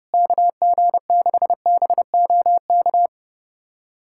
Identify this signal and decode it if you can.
Morse code, heard as a keyed tone, about 30 wpm, 700 Hz. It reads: KG6BOK
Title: USA Callsign DE Callsign - 6 Characters - 30wpm